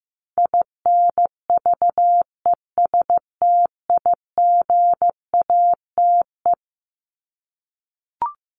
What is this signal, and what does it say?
Morse code, heard as a keyed tone, about 15 wpm, 700 Hz. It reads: INVESTIGATE
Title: Top 1751-2000 Words 15wpm